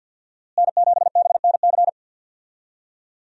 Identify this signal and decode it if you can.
Morse code, heard as a keyed tone, about 50 wpm, 700 Hz. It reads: N7BNC
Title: USA Call Signs - 5 Characters - Repeated 2x - 50wpm